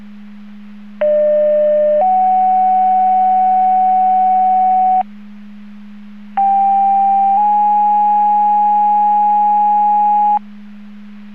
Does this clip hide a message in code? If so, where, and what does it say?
Yes: two-tone page from 1.0 to 10.4 s, heard as a one-second tone, then a three-second tone, about 600 then 750 Hz, about 790 then 830 Hz.